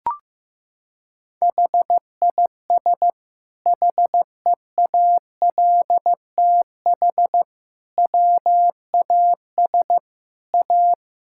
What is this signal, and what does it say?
Morse code, heard as a keyed tone, about 15 wpm, 700 Hz. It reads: HIS HEALTH WAS A
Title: Sentences from Top 1600 Words 15wpm